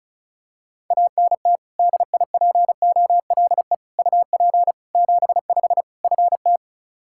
Morse code, heard as a keyed tone, about 36 wpm, 700 Hz. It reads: ANT DIPOLE UP 75 FT
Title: Practice QSO - Exchange 2 - Info - 35wpm